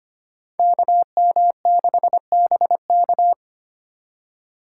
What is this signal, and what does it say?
Morse code, heard as a keyed tone, about 25 wpm, 700 Hz. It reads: KM6BK
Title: USA Call Signs - 5 Characters - 4x Character Spacing - 25wpm